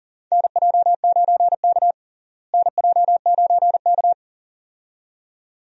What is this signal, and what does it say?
Morse code, heard as a keyed tone, about 40 wpm, 700 Hz. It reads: NJ9K NJ9K
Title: USA Call Signs - 4 Characters - Repeated 2x - 40wpm